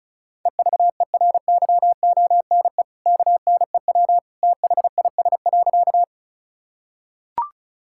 Morse code, heard as a keyed tone, about 36 wpm, 700 Hz. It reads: EVERYONE KNEW THIS.
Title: Sentences from Top 1000 Words - Review Part 5 - 35wpm